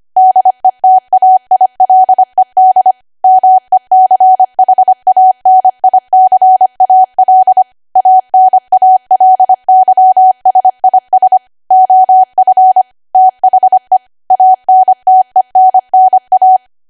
Morse code, heard as a keyed tone, about 25 wpm, 750 Hz. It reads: DETAILED MECHANICAL ANALYSIS OF THE ANTENNA
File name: Archive / 25 WPM / 25WPM